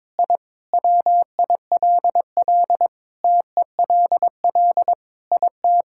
Morse code, heard as a keyed tone, about 21 wpm, 700 Hz. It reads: I WILL TELL IT
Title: Sentences from Top 1000 Words - Review Part 5 - 22wpm